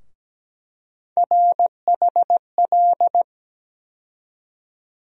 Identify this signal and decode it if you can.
Morse code, heard as a keyed tone, about 17 wpm, 700 Hz. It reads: RHL
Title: USA Call Sign Postfixes - 3 Letters 17wpm